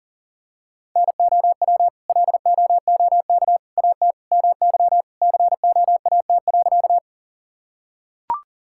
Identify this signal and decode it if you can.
Morse code, heard as a keyed tone, about 40 wpm, 700 Hz. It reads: NOW LOOK AT MY COAT.
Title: Sentences from Top 1600 Words 40wpm